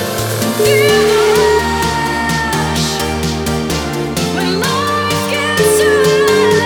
Ringback tone (Europe, 425 Hz), with noise.